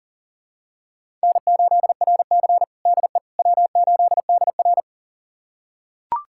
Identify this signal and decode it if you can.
Morse code, heard as a keyed tone, about 37 wpm, 700 Hz. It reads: N8RC DE W8DR
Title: USA Callsign DE Callsign - 4 Characters - 40wpm